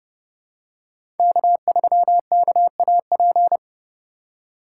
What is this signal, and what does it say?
Morse code, heard as a keyed tone, about 32 wpm, 700 Hz. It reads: K3KAP
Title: USA Call Signs - 2x Character Spacing - 30wpm